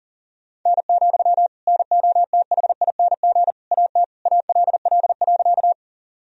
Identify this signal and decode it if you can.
Morse code, heard as a keyed tone, about 40 wpm, 700 Hz. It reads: N, NOTHING AT ALL.
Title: Sentences from Top 500 Words and Abbreviations 40wpm